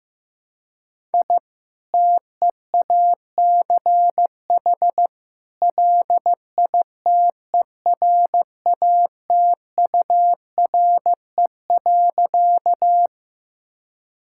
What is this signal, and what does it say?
Morse code, heard as a keyed tone, about 15 wpm, 700 Hz. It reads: I TEACH LITERATURE.
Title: Sentences from Top 1000 Words - Review Part 6 - 15wpm